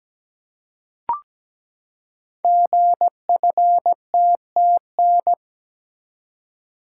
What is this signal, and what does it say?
Morse code, heard as a keyed tone, about 17 wpm, 700 Hz. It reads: GFTTN
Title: Code Groups - 5 Letters - 17wpm